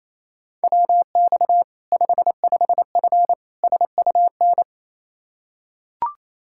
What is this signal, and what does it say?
Morse code, heard as a keyed tone, about 28 wpm, 700 Hz. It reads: WX 55F SUN
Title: Practice QSO - Exchange 2 - Info - 28wpm